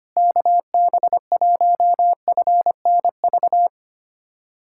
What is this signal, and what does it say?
Morse code, heard as a keyed tone, about 25 wpm, 700 Hz. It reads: KB1FNV
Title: USA Callsign DE Callsign - 6 Characters - 25wpm